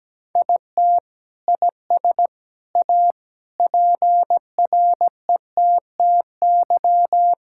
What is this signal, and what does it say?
Morse code, heard as a keyed tone, about 17 wpm, 700 Hz. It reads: IT IS A PRETTY
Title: Sentences from Top 1000 Words - Review Part 6 - 17wpm